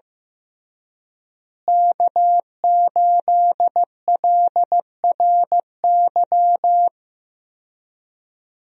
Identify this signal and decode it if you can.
Morse code, heard as a keyed tone, about 15 wpm, 700 Hz. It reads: K8LRY
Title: USA Call Signs - 5 Characters - Repeated 3x - 15wpm